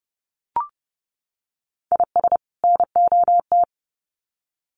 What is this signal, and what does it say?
Morse code, heard as a keyed tone, about 30 wpm, 700 Hz. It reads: IS NOT
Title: Sets of 2 Words 30wpm